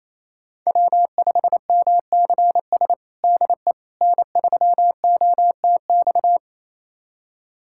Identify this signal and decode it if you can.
Morse code, heard as a keyed tone, about 28 wpm, 700 Hz. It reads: W5MCS DE N3OTX
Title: USA Callsign DE Callsign - 5 Characters - 28wpm